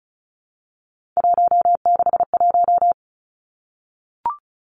Morse code, heard as a keyed tone, about 36 wpm, 700 Hz. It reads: 161
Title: Three Challenging Letter-Number Combinations 35wpm